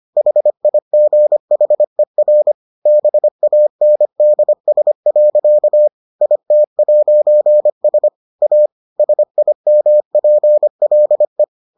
Morse code, heard as a keyed tone, about 25 wpm, 600 Hz. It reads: HIGHER BANDS. IT'S A SIMPLE